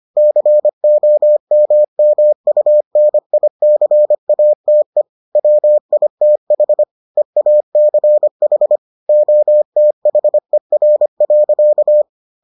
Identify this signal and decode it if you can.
Morse code, heard as a keyed tone, about 25 wpm, 600 Hz. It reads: COMMUNICATE WITH EACH OTHER.